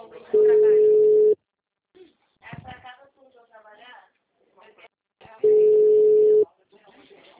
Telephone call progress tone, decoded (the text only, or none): ringback tone